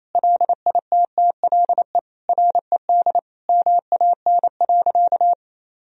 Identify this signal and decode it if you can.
Morse code, heard as a keyed tone, about 28 wpm, 700 Hz. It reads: LITTLE RED MAN.
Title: Sentences from Top 1000 Words - Review Part 2 - 28wpm